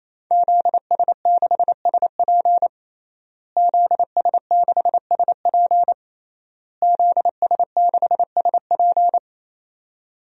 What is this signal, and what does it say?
Morse code, heard as a keyed tone, about 28 wpm, 700 Hz. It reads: ZS6SP ZS6SP ZS6SP
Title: DXCC-100 Call Signs - 5 Characters - Repeated 3x - 28wpm